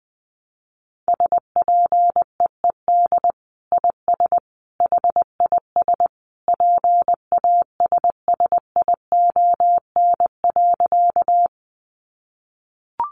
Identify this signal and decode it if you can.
Morse code, heard as a keyed tone, about 20 wpm, 700 Hz. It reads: SPEED IS HIS PASSION.
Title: Sentences from Top 1300 Words 20wpm